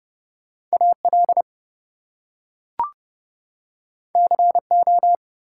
Morse code, heard as a keyed tone, about 30 wpm, 700 Hz. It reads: AL CO
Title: US States and Canadian Provinces 30wpm